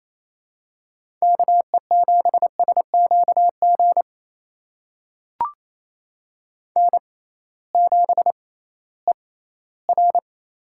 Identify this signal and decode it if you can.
Morse code, heard as a keyed tone, about 28 wpm, spaced as for 16 wpm, 700 Hz. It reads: KE7SQG N7ER
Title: USA Call Signs - 6x Character Spacing - 28wpm